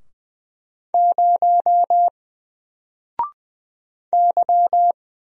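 Morse code, heard as a keyed tone, about 20 wpm, 700 Hz. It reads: 0 Y
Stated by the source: Single letter and number Morse drill at 20 wpm, from N4LCD.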